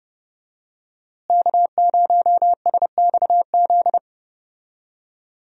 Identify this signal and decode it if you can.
Morse code, heard as a keyed tone, about 30 wpm, 700 Hz. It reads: K0SXZ
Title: USA Call Signs - 5 Characters - Repeated 3x - 30wpm